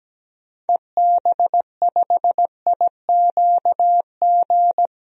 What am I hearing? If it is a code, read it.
Morse code, heard as a keyed tone, about 17 wpm, 700 Hz. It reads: EB5IQG
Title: DXCC-100 Call Signs - 6 Characters - 17wpm